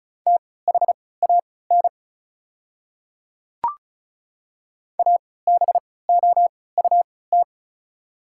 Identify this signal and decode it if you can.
Morse code, heard as a keyed tone, about 36 wpm, spaced as for 20 wpm, 690 Hz. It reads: THAN ABOUT
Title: 3x Character Spacing - Top 500 Words - 35wpm